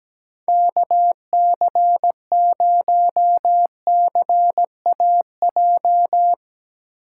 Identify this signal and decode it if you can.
Morse code, heard as a keyed tone, about 20 wpm, 700 Hz. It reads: KC0CAJ
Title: Speed Racing - USA 2x3 Call Signs 17wpm